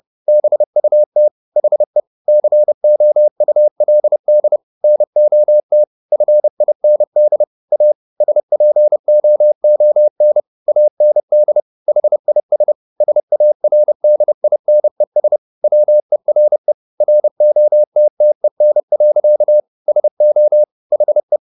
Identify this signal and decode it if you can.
Morse code, heard as a keyed tone, about 30 wpm, 600 Hz. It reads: BUT HE COULD NOT FIND A SPOON AND HIS SARDINES WERE ROTTEN. SO HE